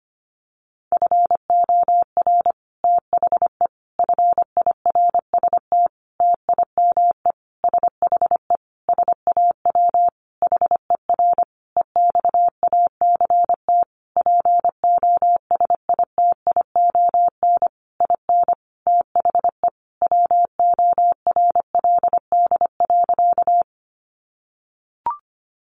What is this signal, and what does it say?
Morse code, heard as a keyed tone, about 25 wpm, 700 Hz. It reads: FOR THE FIRST TIME SHE SAW HER EXACT POSITION IN THE WORLD.
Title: Sentences from Top 1300 Words 25wpm